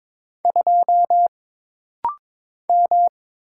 Morse code, heard as a keyed tone, about 22 wpm, 700 Hz. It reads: 2 M
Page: This rapid-fire timing is geared to push your ICR (Instant Character Recognition) skills.